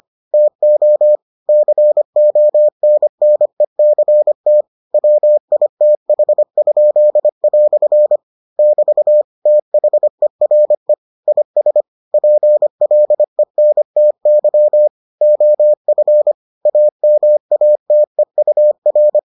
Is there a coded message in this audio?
Yes: Morse code from 0.3 to 19.2 s, heard as a keyed tone, about 25 wpm, 600 Hz. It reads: TO CONNECT WITH?" <BT> THERE IS PLENTY OF AMATEUR